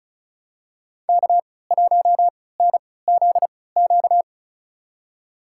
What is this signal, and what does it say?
Morse code, heard as a keyed tone, about 36 wpm, spaced as for 20 wpm, 700 Hz. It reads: K1NZQ
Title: USA Call Signs - 5 Characters - 3x Character Spacing - 35wpm